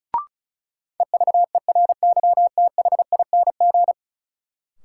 Morse code, heard as a keyed tone, about 36 wpm, 700 Hz. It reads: EVERYTHING
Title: Sentences from Top 700 Words - New Word Drill - 35wpm